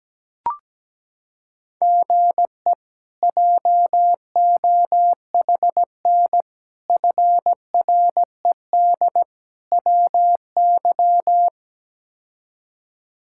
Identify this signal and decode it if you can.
Morse code, heard as a keyed tone, about 17 wpm, 700 Hz. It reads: GE JOHN FRED WY
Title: K1USN SST QSO Practice - 17wpm